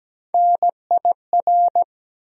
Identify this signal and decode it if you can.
Morse code, heard as a keyed tone, about 17 wpm, 700 Hz. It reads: NIR
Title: USA Call Sign Postfixes - 3 Letters 17wpm